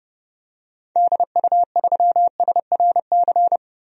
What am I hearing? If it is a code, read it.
Morse code, heard as a keyed tone, about 30 wpm, 700 Hz. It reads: DU3SRC
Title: DXCC-100 Call Signs - 6 Characters - 30wpm